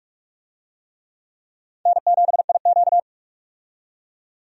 Morse code, heard as a keyed tone, about 46 wpm, 700 Hz. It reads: N7IQ
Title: USA Call Signs - 4 Characters - 45wpm